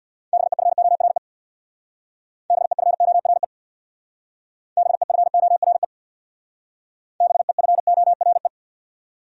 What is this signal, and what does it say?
Morse code, heard as a keyed tone, about 64 wpm, 700 Hz. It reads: BEFORE BEFORE BEFORE BEFORE